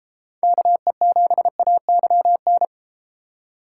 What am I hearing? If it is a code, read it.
Morse code, heard as a keyed tone, about 33 wpm, 700 Hz. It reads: KE7AYN